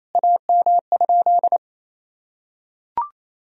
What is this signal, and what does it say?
Morse code, heard as a keyed tone, about 28 wpm, 700 Hz. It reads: AM?
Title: Sentences from Top 1000 Words - Review Part 2 - 28wpm